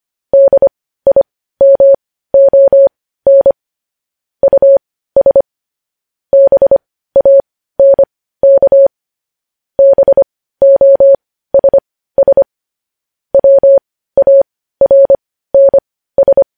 Morse code, heard as a keyed tone, about 25 wpm, spaced as for 15 wpm, 550 Hz. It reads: DIMON US BANK BOSS WARNS